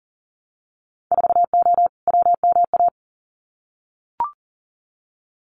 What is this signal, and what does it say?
Morse code, heard as a keyed tone, about 40 wpm, 700 Hz. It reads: 4O WMA
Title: Winter Field Day Exchanges Encore 40wpm